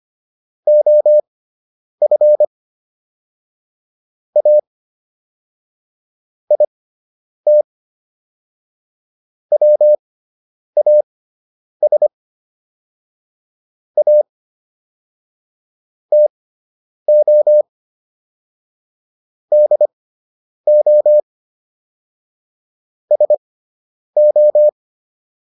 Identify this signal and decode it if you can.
Morse code, heard as a keyed tone, about 25 wpm, spaced as for 9 wpm, 600 Hz. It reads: OF A IT WAS A TO DO SO